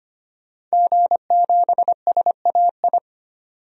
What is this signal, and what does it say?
Morse code, heard as a keyed tone, about 25 wpm, 700 Hz. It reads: G7SAI
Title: DXCC-100 Call Signs - 5 Characters - 25wpm